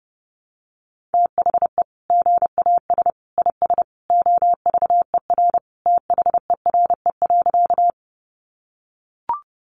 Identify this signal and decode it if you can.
Morse code, heard as a keyed tone, about 30 wpm, 700 Hz. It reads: THE GAS IS OVER THERE.
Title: Sentences from Top 1100 Words 30wpm